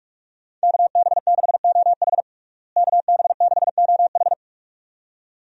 Morse code, heard as a keyed tone, about 42 wpm, 700 Hz. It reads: KB6OH KB6OH
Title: USA Call Signs - 5 Characters - Repeated 2x - 45wpm